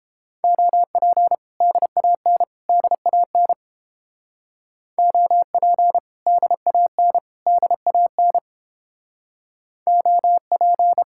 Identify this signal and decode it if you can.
Morse code, heard as a keyed tone, about 30 wpm, 700 Hz. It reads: OP DAN DAN OP DAN DAN OP